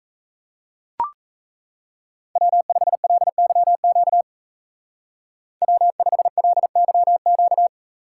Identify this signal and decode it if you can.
Morse code, heard as a keyed tone, about 34 wpm, 700 Hz. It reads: W5LYQ W5LYQ
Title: Speed Racing - USA Call Signs 28wpm